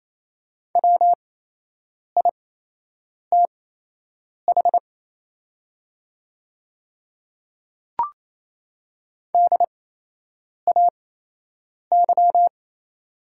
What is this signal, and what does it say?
Morse code, heard as a keyed tone, about 28 wpm, spaced as for 8 wpm, 700 Hz. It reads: WITH DAY
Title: 8x Character Spacing - Top 100 Words - 28wpm